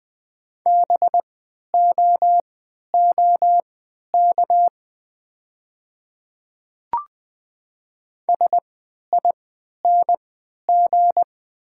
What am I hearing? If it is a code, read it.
Morse code, heard as a keyed tone, about 20 wpm, spaced as for 11 wpm, 690 Hz. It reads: BOOK SING
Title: 3x Character Spacing - Top 500 Words - 20wpm